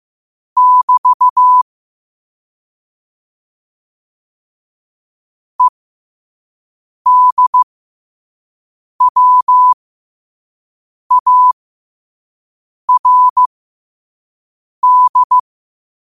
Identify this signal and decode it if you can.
Morse code, heard as a keyed tone, about 15 wpm, spaced as for 5 wpm, 1000 Hz. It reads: <BT> EDWARD
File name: qotdfiles / 5wpm